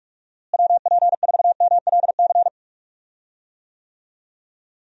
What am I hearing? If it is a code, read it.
Morse code, heard as a keyed tone, about 42 wpm, 700 Hz. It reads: WP4MLC
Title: USA Call Signs - 6 Characters - Repeated 2x - 45wpm